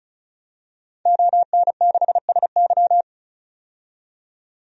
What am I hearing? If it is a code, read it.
Morse code, heard as a keyed tone, about 36 wpm, 700 Hz. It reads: ON6SY
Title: DXCC-100 Call Signs - 5 Characters - 35wpm